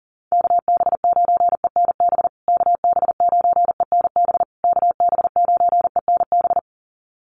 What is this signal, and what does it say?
Morse code, heard as a keyed tone, about 40 wpm, 700 Hz. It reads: KB9ENB KB9ENB KB9ENB
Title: USA Call Signs Repeated 3x 40wpm